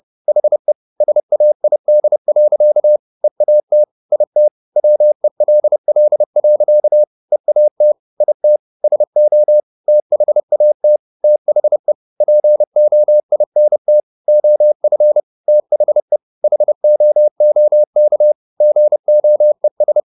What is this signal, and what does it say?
Morse code, heard as a keyed tone, about 30 wpm, 600 Hz. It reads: HE SAID. EAT IT WELL. EAT IT SO THAT THE POINT OF THE HOOK GOES
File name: the old man and the sea / the old man and the sea 1 30wpm